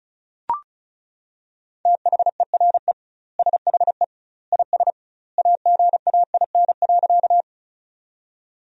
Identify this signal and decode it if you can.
Morse code, heard as a keyed tone, about 34 wpm, 700 Hz. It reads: THERE SHE IS AGAIN.
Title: 2x Word-Spacing - Sentences from Top 200 Words - 35wpm